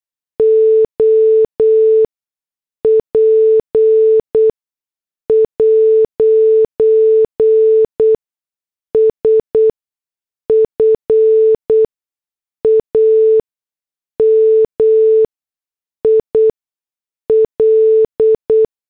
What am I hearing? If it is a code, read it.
Morse code, heard as a keyed tone, about 8 wpm, spaced as for 6 wpm, 440 Hz. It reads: OP'S FAMIL